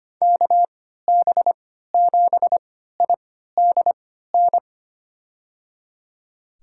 Morse code, heard as a keyed tone, about 25 wpm, spaced as for 14 wpm, 700 Hz. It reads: KB7IDN
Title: USA Call Signs - 6 Characters - 3x Character Spacing - 25wpm